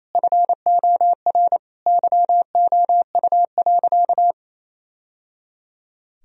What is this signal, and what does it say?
Morse code, heard as a keyed tone, about 28 wpm, 700 Hz. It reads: FOR YOU.
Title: Sentences from Top 1000 Words - Review Part 4 - 28wpm